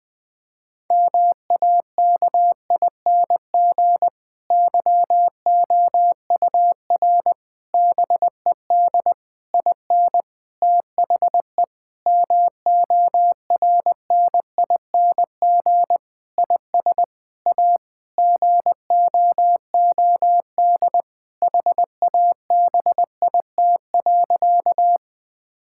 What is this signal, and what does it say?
Morse code, heard as a keyed tone, about 20 wpm, 700 Hz. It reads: MAKING YOUR BED IN THE MORNING IS A GOOD HABIT.
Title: Difficult Sentences with Hints - 20wpm